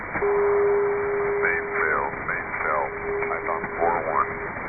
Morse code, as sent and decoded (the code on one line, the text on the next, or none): none